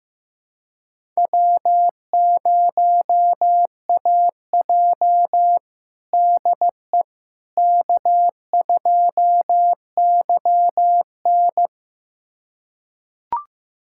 Morse code, heard as a keyed tone, about 15 wpm, 700 Hz. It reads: W0AJ DE K2YN
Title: USA Callsign DE Callsign - 4 Characters - 15wpm